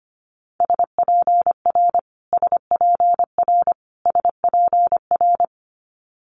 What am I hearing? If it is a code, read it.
Morse code, heard as a keyed tone, about 25 wpm, 700 Hz. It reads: SPR SPR SPR
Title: Multi-letter Phonemes Repeated 3x 25wpm